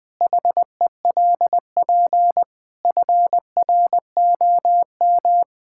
Morse code, heard as a keyed tone, about 20 wpm, 700 Hz. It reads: HELP FROM
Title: Sentences with 7-Word Phrases - Spoken Beginning - 20wpm